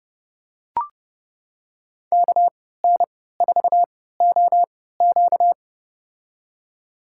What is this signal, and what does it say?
Morse code, heard as a keyed tone, about 30 wpm, spaced as for 17 wpm, 700 Hz. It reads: KN4OQ
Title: USA Call Signs - 5 Characters - 3x Character Spacing - 30wpm